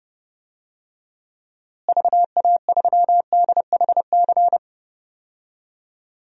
Morse code, heard as a keyed tone, about 30 wpm, 700 Hz. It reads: VA3DHC
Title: DXCC-100 Call Signs - 6 Characters - Repeated 2x - 30wpm